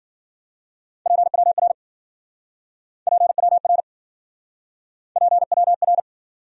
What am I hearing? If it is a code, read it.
Morse code, heard as a keyed tone, about 52 wpm, 700 Hz. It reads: PWR PWR PWR